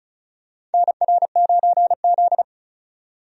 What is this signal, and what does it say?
Morse code, heard as a keyed tone, about 36 wpm, 700 Hz. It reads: NR9Z
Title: USA Call Signs - 4 Characters - 35wpm